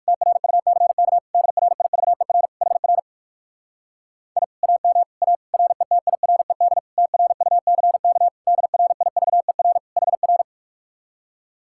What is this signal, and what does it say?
Morse code, heard as a keyed tone, about 59 wpm, 700 Hz. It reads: TRUCK DRIVER HR I AM A RETIRED TRUCK DRIVER HR